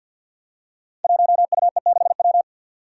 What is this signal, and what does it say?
Morse code, heard as a keyed tone, about 51 wpm, 700 Hz. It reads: 1UE6W
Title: Five Challenging Letter-Number Combinations 50wpm